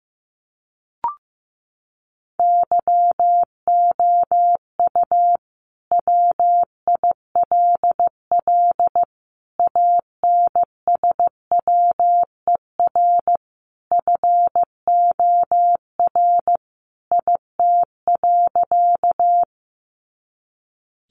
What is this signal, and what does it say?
Morse code, heard as a keyed tone, about 15 wpm, 700 Hz. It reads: YOU WILL ANSWER FOR IT.
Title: Sentences from Top 300 Words 15wpm